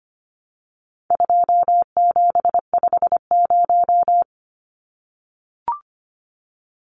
Morse code, heard as a keyed tone, about 25 wpm, 700 Hz. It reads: 2750
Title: Numbers - 4-Digit 25wpm